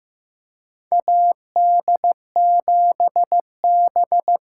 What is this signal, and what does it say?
Morse code, heard as a keyed tone, about 15 wpm, 700 Hz. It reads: AD7B
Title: USA Call Signs - 5 Characters - 6x Character Spacing - 15wpm